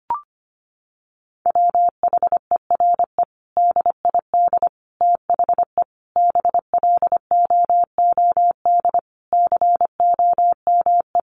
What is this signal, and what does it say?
Morse code, heard as a keyed tone, about 25 wpm, 700 Hz. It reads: WHERE DID THE BLOOD COME
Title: Sentences from Top 1100 Words 25wpm